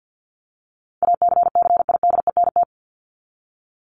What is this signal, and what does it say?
Morse code, heard as a keyed tone, about 48 wpm, 700 Hz. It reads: ACCIDENT
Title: Top 1751-2000 Words 50wpm